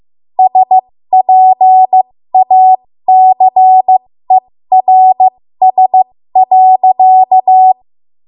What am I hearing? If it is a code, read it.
Morse code, heard as a keyed tone, about 15 wpm, spaced as for 13 wpm, 750 Hz. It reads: SPACERS.
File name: Morse / 13WPM